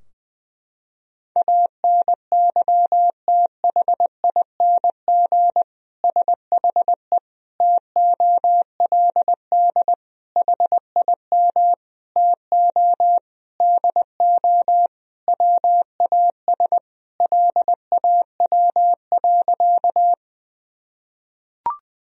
Morse code, heard as a keyed tone, about 20 wpm, 700 Hz. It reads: ANYTHING SHE TOLD HIM TO DO WAS LAW.
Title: Sentences from Top 1000 Words - Review Part 3 - 20wpm